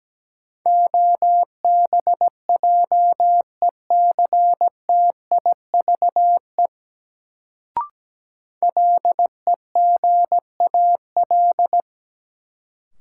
Morse code, heard as a keyed tone, about 17 wpm, 700 Hz. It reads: OBJECTIVE LEGAL
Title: Sentences from Top 1200 Words - New Word Drill Alt - 17wpm